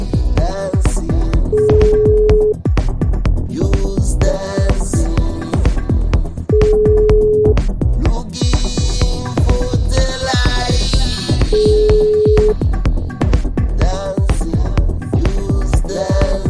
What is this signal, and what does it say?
Ringback tone (Europe, 425 Hz), with noise.